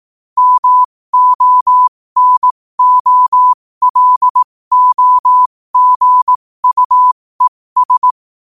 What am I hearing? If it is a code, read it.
Morse code, heard as a keyed tone, about 18 wpm, spaced as for 15 wpm, 1000 Hz. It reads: MONOLOGUES
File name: qotdfiles / 15wpm